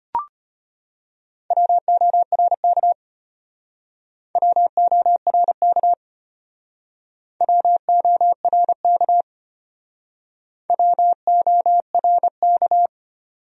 Morse code, heard as a keyed tone, about 32 wpm, 700 Hz. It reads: WORK WORK WORK WORK